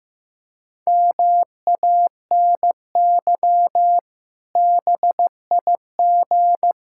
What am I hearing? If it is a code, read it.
Morse code, heard as a keyed tone, about 15 wpm, 700 Hz. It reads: MANY BIG
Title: Pangram Challenge v2 - 15wpm